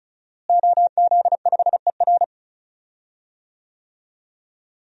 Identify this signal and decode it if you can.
Morse code, heard as a keyed tone, about 36 wpm, 700 Hz. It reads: OZ5ER
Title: DXCC-100 Call Signs - 5 Characters - Repeated 3x - 35wpm